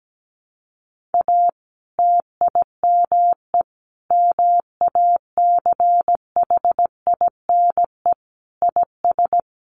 Morse code, heard as a keyed tone, about 17 wpm, 700 Hz. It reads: A TIME MACHINE IS
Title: A Sound Of Thunder 17wpm